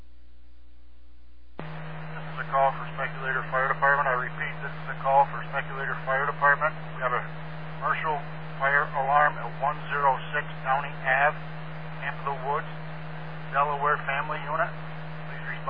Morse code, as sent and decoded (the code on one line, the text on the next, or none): none